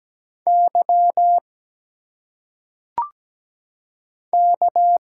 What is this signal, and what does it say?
Morse code, heard as a keyed tone, about 17 wpm, 700 Hz. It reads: Y K
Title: Single Letters - Phonetic Pronunciation 17wpm